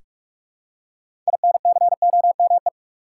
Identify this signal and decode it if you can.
Morse code, heard as a keyed tone, about 46 wpm, 700 Hz. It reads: INCOME
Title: Six Letter Words - 45wpm